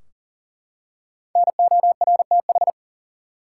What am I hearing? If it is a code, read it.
Morse code, heard as a keyed tone, about 40 wpm, 700 Hz. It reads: NORTH
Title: Five Letter Words - 40wpm